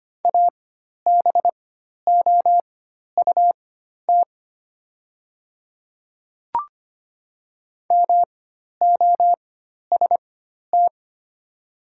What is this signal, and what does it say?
Morse code, heard as a keyed tone, about 25 wpm, spaced as for 12 wpm, 700 Hz. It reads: ABOUT MOST
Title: 4x Character Spacing - Top 100 Words - 25wpm